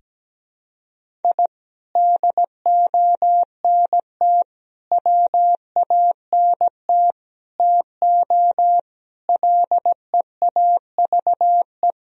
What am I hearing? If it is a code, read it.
Morse code, heard as a keyed tone, about 17 wpm, 700 Hz. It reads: I DONT WANT TO LEAVE
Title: Sentences from Top 1000 Words - Review Part 1 - 17wpm